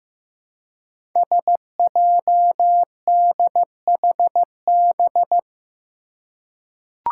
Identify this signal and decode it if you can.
Morse code, heard as a keyed tone, about 15 wpm, 700 Hz. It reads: SJDHB
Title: Five Challenging Letter-Number Combinations 15wpm